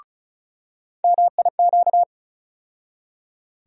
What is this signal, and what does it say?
Morse code, heard as a keyed tone, about 34 wpm, 700 Hz. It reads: MIQ
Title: USA Call Sign Postfixes - 3 Letters 35wpm